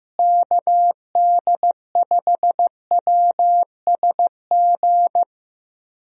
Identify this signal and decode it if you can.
Morse code, heard as a keyed tone, about 15 wpm, 700 Hz. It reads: KD5WSG
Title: USA Call Signs - 6 Characters - Repeated 3x - 15wpm